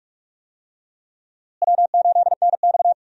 Morse code, heard as a keyed tone, about 46 wpm, 700 Hz. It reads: W8NX
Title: USA Call Signs - 4 Characters - Repeated 3x - 45wpm